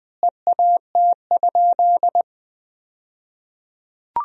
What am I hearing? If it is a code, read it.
Morse code, heard as a keyed tone, about 21 wpm, 700 Hz. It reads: EAT?
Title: 2x Word-Spacing - Sentences from Top 300 Words - 20wpm